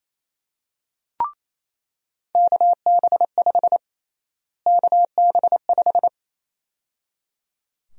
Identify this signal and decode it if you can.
Morse code, heard as a keyed tone, about 28 wpm, 700 Hz. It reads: KB5 KB5
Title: USA Call Sign Prefixes - Repeated 2x - 28wpm